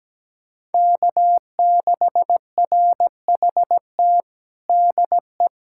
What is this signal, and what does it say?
Morse code, heard as a keyed tone, about 17 wpm, 700 Hz. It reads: K6RHT DE
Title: K1USN SST QSO Practice 17wpm